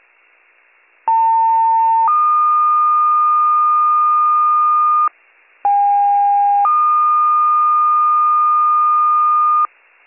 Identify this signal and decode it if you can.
Two-tone page, heard as a one-second tone, then a three-second tone, about 900 then 1220 Hz, about 810 then 1220 Hz. It is heard from 1.1 to 9.7 s.